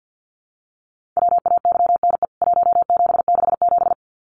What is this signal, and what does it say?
Morse code, heard as a keyed tone, about 50 wpm, 700 Hz. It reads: WAYNE 1767
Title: CWT Exchange 50wpm